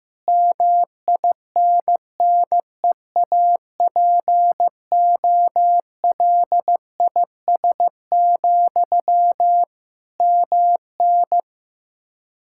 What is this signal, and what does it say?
Morse code, heard as a keyed tone, about 15 wpm, 700 Hz. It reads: MINNEAPOLIS, MN
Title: Practice QSO - Exchange 1 - Beginning - 15wpm